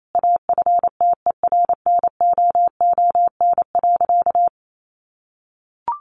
Morse code, heard as a keyed tone, about 28 wpm, 700 Hz. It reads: AFTERNOON.
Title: Sentences from Top 1500 Words 28wpm